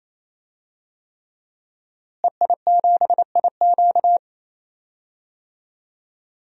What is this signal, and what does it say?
Morse code, heard as a keyed tone, about 28 wpm, 700 Hz. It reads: EI7IQ
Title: DXCC-100 Call Signs - 5 Characters - Repeated 3x - 28wpm